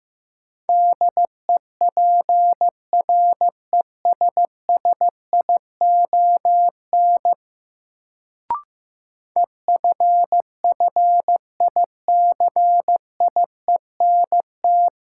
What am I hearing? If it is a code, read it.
Morse code, heard as a keyed tone, about 15 wpm, 700 Hz. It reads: DEPRESSION EFFICIENT
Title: Sentences from Top 1300 Words - New Word Drill - 15wpm